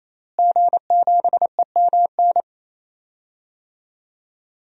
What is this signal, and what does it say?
Morse code, heard as a keyed tone, about 27 wpm, 700 Hz. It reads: G7EMN
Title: DXCC-100 Call Signs - 5 Characters - Repeated 3x - 28wpm